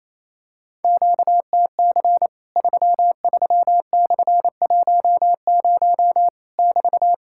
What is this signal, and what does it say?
Morse code, heard as a keyed tone, about 28 wpm, 700 Hz. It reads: QTC 33/10 <BT>
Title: Worked All Europe DX Contest - QTC Exchange - 28wpm